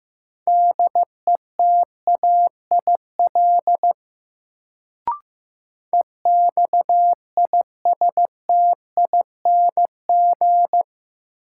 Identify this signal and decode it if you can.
Morse code, heard as a keyed tone, about 15 wpm, 700 Hz. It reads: DETAIL EXISTING
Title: Sentences from Top 1500 Words - New Word Drill Alt - 15wpm